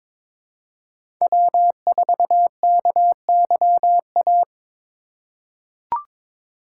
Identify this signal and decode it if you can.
Morse code, heard as a keyed tone, about 22 wpm, 700 Hz. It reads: W4KYA
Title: USA Call Signs - 5 Characters - 3x Character Spacing - 22wpm